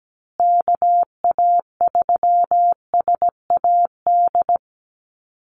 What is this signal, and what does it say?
Morse code, heard as a keyed tone, about 17 wpm, 700 Hz. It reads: KA3SAD
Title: USA Call Signs Repeated 2x 17wpm